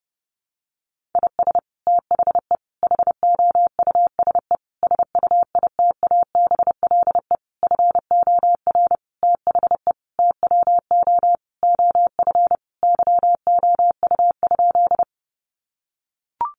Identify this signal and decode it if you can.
Morse code, heard as a keyed tone, about 30 wpm, 700 Hz. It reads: IS THE HOUSE SUITABLE FOR THE TWO OF YOU?
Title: Sentences from Top 1400 Words 30wpm